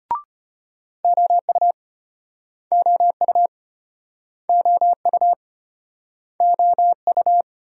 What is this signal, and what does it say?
Morse code, heard as a keyed tone, about 31 wpm, 700 Hz. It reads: OU OU OU OU
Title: Speed-Racing Multi-letter Phonemes - 25wpm